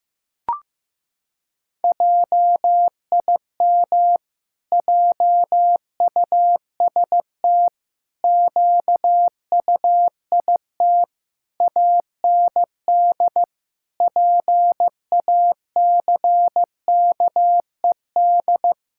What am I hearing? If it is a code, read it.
Morse code, heard as a keyed tone, about 15 wpm, 700 Hz. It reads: JIM JUST QUIT AND PACKED
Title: Pangram Challenge v2 - 15wpm